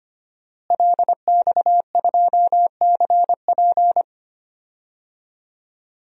Morse code, heard as a keyed tone, about 25 wpm, 700 Hz. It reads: LX2CP
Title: DXCC-100 Call Signs - 5 Characters - Repeated 3x - 25wpm